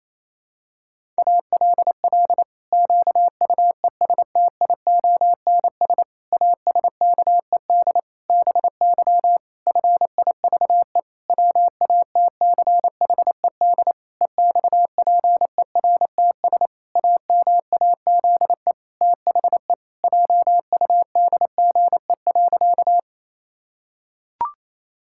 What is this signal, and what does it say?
Morse code, heard as a keyed tone, about 28 wpm, 700 Hz. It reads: ALL QUESTIONS ASKED BY FIVE WATCHED EXPERTS AMAZE THE JUDGE.
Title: Pangram Challenge v2 - 28wpm